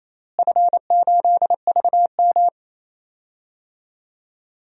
Morse code, heard as a keyed tone, about 28 wpm, 700 Hz. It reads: F8VM
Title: DXCC-100 Call Signs - 4 Characters - Repeated 3x - 28wpm